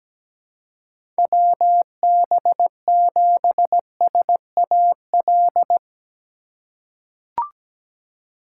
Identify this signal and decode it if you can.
Morse code, heard as a keyed tone, about 17 wpm, 700 Hz. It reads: WB7SAL
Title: USA Call Signs - 6 Characters - 2x Character Spacing - 17wpm